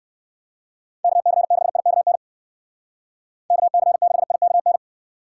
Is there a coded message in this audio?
Yes: Morse code from 1.0 to 4.8 s, heard as a keyed tone, about 59 wpm, 700 Hz. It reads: KC6IKN KC6IKN